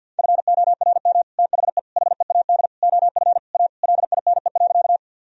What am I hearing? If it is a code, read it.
Morse code, heard as a keyed tone, about 51 wpm, 700 Hz. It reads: FORM THE HEAD OF A LINE.
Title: Sentences from Top 1000 Words - Review Part 1 - 50wpm